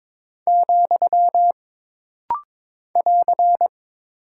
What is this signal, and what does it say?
Morse code, heard as a keyed tone, about 22 wpm, 700 Hz. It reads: , <AR>
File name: practice / 22wpm / Single Punctuation-Prosign - Rapid-Fire - 22wpm